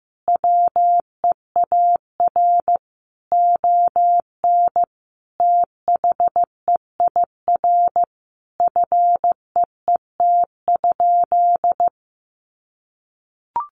Morse code, heard as a keyed tone, about 15 wpm, 700 Hz. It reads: WEAR ON THEIR FEET?
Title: Sentences from Top 1500 Words 15wpm